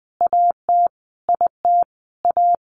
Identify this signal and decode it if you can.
Morse code, heard as a keyed tone, about 20 wpm, 700 Hz. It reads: AT IT A
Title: Sentences from Top 1000 Words 20wpm